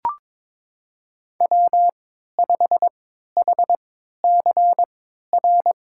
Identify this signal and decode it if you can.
Morse code, heard as a keyed tone, about 22 wpm, spaced as for 13 wpm, 700 Hz. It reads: W5HCR
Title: USA Call Signs - 5 Characters - 3x Character Spacing - 22wpm